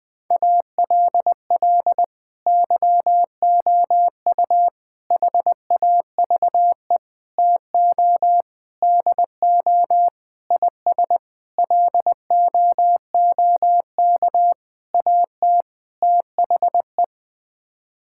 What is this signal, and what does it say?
Morse code, heard as a keyed tone, about 20 wpm, 700 Hz. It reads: ALL YOU HAVE TO DO IS LOOK AT THE
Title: Sets of 9 Words - 20wpm